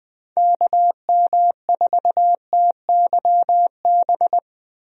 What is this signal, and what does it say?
Morse code, heard as a keyed tone, about 21 wpm, 700 Hz. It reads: KM4TYB
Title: USA Call Signs - 6 Characters - Repeated 2x - 20wpm